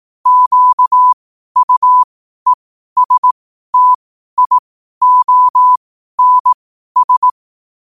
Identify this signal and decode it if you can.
Morse code, heard as a keyed tone, about 18 wpm, spaced as for 12 wpm, 1000 Hz. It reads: QUESTIONS
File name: qotdfiles / 12wpm